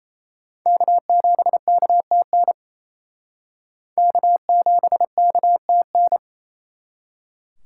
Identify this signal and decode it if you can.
Morse code, heard as a keyed tone, about 37 wpm, 700 Hz. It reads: K7KTN K7KTN